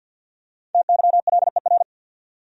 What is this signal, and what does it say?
Morse code, heard as a keyed tone, about 52 wpm, 700 Hz. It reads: TYLER
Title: USA Common Names - Max 5 Letters - 50wpm